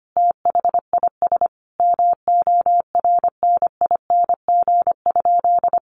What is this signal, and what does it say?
Morse code, heard as a keyed tone, about 25 wpm, 700 Hz. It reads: THIS MORNING?
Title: Sentences from Top 1200 Words 25wpm